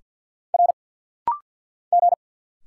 Morse code, heard as a keyed tone, about 48 wpm, 700 Hz. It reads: R G
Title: Single Letter-Number - Rapid-Fire 50wpm